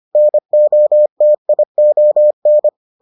Morse code, heard as a keyed tone, about 25 wpm, 600 Hz. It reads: NOTION